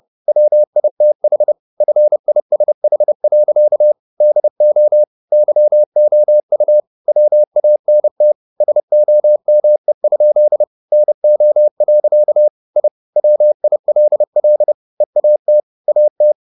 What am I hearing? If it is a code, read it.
Morse code, heard as a keyed tone, about 30 wpm, 600 Hz. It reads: WITH FISH. DO YOU WANT SOME? NO. I WILL EAT AT
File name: the old man and the sea / the old man and the sea 1 30wpm